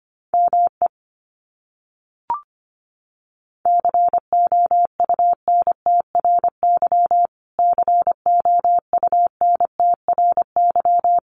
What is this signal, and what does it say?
Morse code, heard as a keyed tone, about 25 wpm, 700 Hz. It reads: ME COUNTRY COUNTRY
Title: Top 300 Words Repeated 25wpm